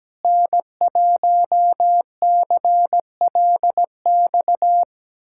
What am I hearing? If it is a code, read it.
Morse code, heard as a keyed tone, about 17 wpm, 700 Hz. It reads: N1CLX
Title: USA Callsign DE Callsign - 5 Characters - 17wpm